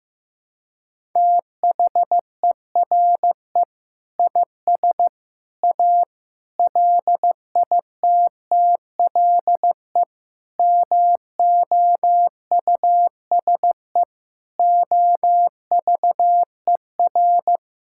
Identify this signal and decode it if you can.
Morse code, heard as a keyed tone, about 15 wpm, 700 Hz. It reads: THERE IS A LITTLE MOUSE OVER
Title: Sentences from Top 1000 Words - Review Part 5 - 15wpm